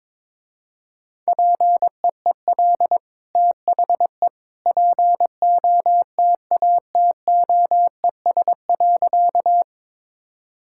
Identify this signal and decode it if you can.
Morse code, heard as a keyed tone, about 22 wpm, 700 Hz. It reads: PEEL THE POTATOES.
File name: practice / 22wpm / Easy Sentences with Hints - 22wpm